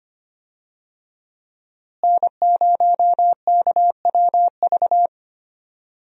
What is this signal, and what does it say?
Morse code, heard as a keyed tone, about 25 wpm, 700 Hz. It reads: N0KWV
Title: USA Call Signs - 5 Characters - Repeated 3x - 25wpm